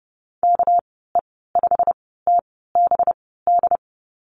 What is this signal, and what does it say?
Morse code, heard as a keyed tone, about 30 wpm, spaced as for 17 wpm, 700 Hz. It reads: KE5TBD
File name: Lesson 37g - Letter x - Call Signs 30wpm